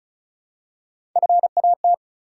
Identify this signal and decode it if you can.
Morse code, heard as a keyed tone, about 36 wpm, 700 Hz. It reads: FAT
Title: Three Letter Words - 35wpm